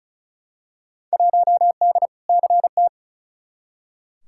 Morse code, heard as a keyed tone, about 34 wpm, 700 Hz. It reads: 1D CT
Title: Field Day Exchanges - 35wpm